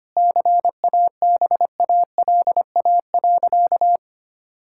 Morse code, heard as a keyed tone, about 25 wpm, 700 Hz. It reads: CABALA.
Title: Pangram Challenge v2 - 25wpm